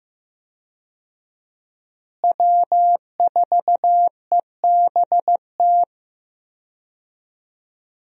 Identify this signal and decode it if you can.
Morse code, heard as a keyed tone, about 15 wpm, 700 Hz. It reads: W4EBT
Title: USA Call Signs - 5 Characters - Repeated 3x - 15wpm